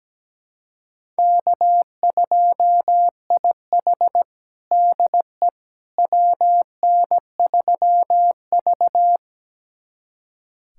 Morse code, heard as a keyed tone, about 17 wpm, 700 Hz. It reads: K2IH DE WN3V
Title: USA Callsign DE Callsign - 4 Characters - 17wpm